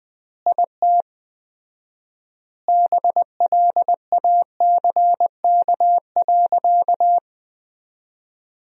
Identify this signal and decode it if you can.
Morse code, heard as a keyed tone, about 20 wpm, 700 Hz. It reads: IT BLACK.
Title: 4x Word-Spacing - Sentences from Top 400 Words - 20wpm